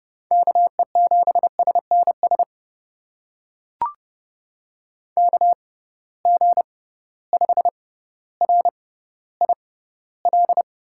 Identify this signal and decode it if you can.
Morse code, heard as a keyed tone, about 30 wpm, spaced as for 15 wpm, 700 Hz. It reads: KE7SNS KG5RIL
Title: USA Call Signs - 6 Characters - 6x Character Spacing - 30wpm